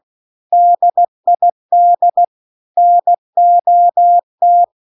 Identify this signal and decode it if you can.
Morse code, heard as a keyed tone, about 16 wpm, 700 Hz. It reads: DID NOT